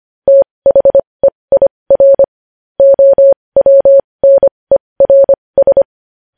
Morse code, heard as a keyed tone, about 25 wpm, spaced as for 20 wpm, 550 Hz. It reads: THEIR OWNERS
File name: podcast / News headlines-1x-25x20